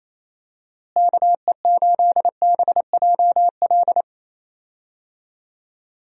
Morse code, heard as a keyed tone, about 28 wpm, 700 Hz. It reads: KE8BJL
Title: USA Call Signs - 6 Characters - Repeated 2x - 28wpm